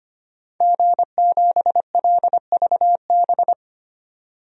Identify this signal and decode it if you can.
Morse code, heard as a keyed tone, about 25 wpm, 700 Hz. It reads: G7LVB
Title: DXCC-100 Call Signs - 5 Characters - 25wpm